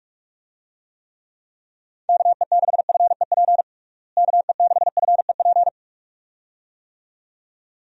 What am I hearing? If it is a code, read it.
Morse code, heard as a keyed tone, about 46 wpm, 700 Hz. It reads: KE6FEP KE6FEP
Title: USA Call Signs - 6 Characters - Repeated 2x - 45wpm